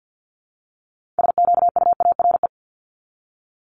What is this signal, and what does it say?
Morse code, heard as a keyed tone, about 50 wpm, 700 Hz. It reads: SQUARE
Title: Multi-letter Phonemes 50wpm